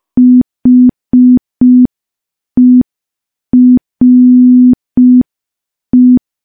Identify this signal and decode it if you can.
Morse code, heard as a keyed tone, about 5 wpm, 250 Hz. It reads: HERE
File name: morse / morse-training-05WPM